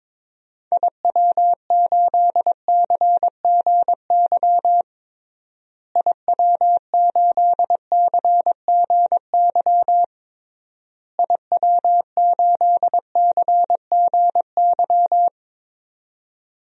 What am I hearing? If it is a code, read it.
Morse code, heard as a keyed tone, about 22 wpm, 700 Hz. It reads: IW8CGY IW8CGY IW8CGY
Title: DXCC-100 Call Signs - 6 Characters - Repeated 3x - 22wpm